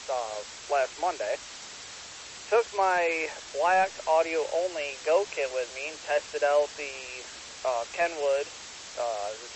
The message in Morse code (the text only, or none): none